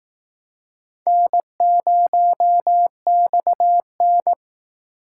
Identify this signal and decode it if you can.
Morse code, heard as a keyed tone, about 19 wpm, 700 Hz. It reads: N0XN